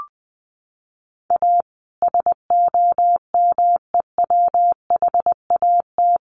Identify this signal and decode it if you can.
Morse code, heard as a keyed tone, about 19 wpm, 700 Hz. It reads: A SOMEWHAT
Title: Sentences from Top 1300 Words 20wpm